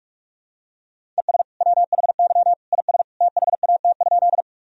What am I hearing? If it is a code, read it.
Morse code, heard as a keyed tone, about 46 wpm, 700 Hz. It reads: ES WHY IS THAT?
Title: Sentences from Top 200 Words and Abbreviations 45wpm